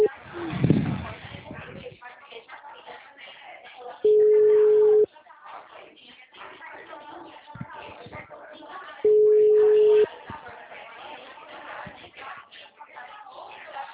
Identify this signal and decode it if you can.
Ringback tone (Europe, 425 Hz), with speech.